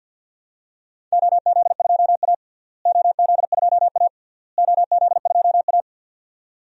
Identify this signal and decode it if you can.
Morse code, heard as a keyed tone, about 53 wpm, 700 Hz. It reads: OZ2A OZ2A OZ2A
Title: DXCC-100 Call Signs - 4 Characters - Repeated 3x - 50wpm